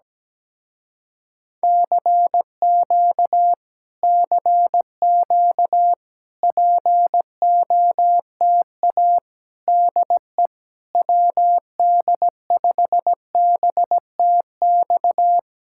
Morse code, heard as a keyed tone, about 17 wpm, 700 Hz. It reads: CQ CQ POTA DE WD5BTX
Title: Speed-Racing Common QSO Phrases - 17wpm